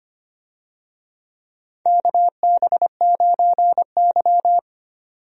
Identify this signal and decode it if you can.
Morse code, heard as a keyed tone, about 25 wpm, 700 Hz. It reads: KB9Y
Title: USA Call Signs - 4 Characters - 25wpm